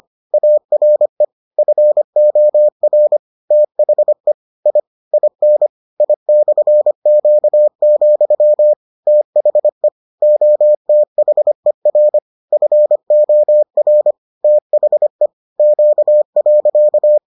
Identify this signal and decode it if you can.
Morse code, heard as a keyed tone, about 25 wpm, 600 Hz. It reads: ARE FOR THE I IN I/Q, THE OTHER FOR THE Q.